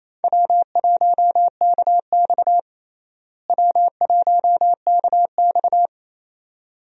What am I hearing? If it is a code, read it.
Morse code, heard as a keyed tone, about 28 wpm, 700 Hz. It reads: W1KX W1KX
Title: USA Call Signs - 4 Characters - Repeated 2x - 28wpm